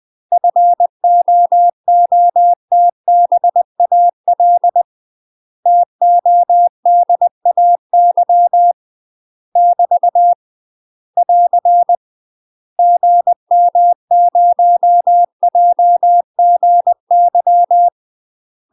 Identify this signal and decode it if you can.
Morse code, heard as a keyed tone, about 20 wpm, 700 Hz. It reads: FOOTBAL TODAY <BT> <AR> GM0JGY
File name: UK MASTER QSOs TRAINING/20WPM